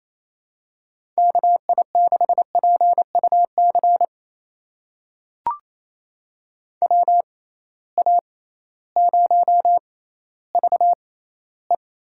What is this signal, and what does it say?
Morse code, heard as a keyed tone, about 27 wpm, spaced as for 15 wpm, 700 Hz. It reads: KI6PUC WA0VE
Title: USA Call Signs - 6 Characters - 6x Character Spacing - 28wpm